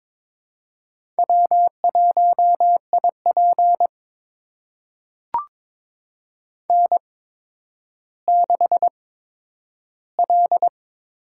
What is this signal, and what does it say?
Morse code, heard as a keyed tone, about 22 wpm, spaced as for 11 wpm, 700 Hz. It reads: W1IP N6L
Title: USA Call Signs - 8x Character Spacing - 22wpm